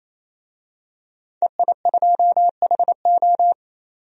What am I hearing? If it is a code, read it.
Morse code, heard as a keyed tone, about 28 wpm, 700 Hz. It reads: EI2HO
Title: DXCC-100 Call Signs - 5 Characters - 28wpm